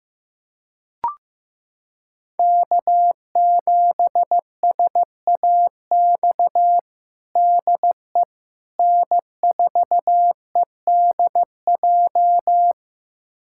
Morse code, heard as a keyed tone, about 16 wpm, 700 Hz. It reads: K7SAX DE N4EDJ
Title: USA Callsign DE Callsign - 5 Characters - 15wpm